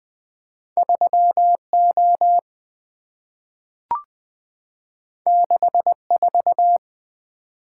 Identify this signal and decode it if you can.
Morse code, heard as a keyed tone, about 20 wpm, 700 Hz. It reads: 3O 64
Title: Two Letter-Number Combinations - 20wpm